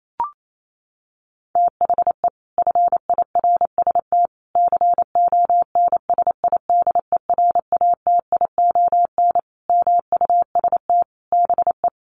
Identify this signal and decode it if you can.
Morse code, heard as a keyed tone, about 28 wpm, 700 Hz. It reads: THE FIRST CONSIDERATION MUST BE
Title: Sentences from Top 1500 Words 28wpm